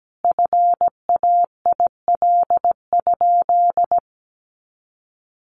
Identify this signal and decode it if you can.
Morse code, heard as a keyed tone, about 17 wpm, 700 Hz. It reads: FAIL?
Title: Sentences from Top 1100 Words 17wpm